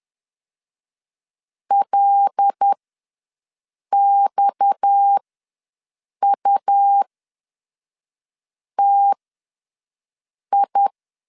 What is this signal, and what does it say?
Morse code, heard as a keyed tone, about 10 wpm, spaced as for 5 wpm, 800 Hz. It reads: LXUTI
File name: LEZIONI / 9 2867 ws 10wpm 4215